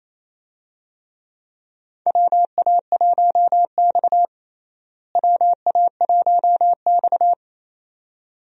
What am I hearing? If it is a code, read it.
Morse code, heard as a keyed tone, about 28 wpm, 700 Hz. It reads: WA1X WA1X
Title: USA Call Signs - 4 Characters - Repeated 2x - 28wpm